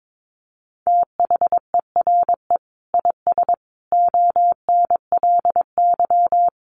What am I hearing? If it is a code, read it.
Morse code, heard as a keyed tone, about 22 wpm, 700 Hz. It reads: THERE IS ONLY
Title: Sentences from Top 1200 Words 22wpm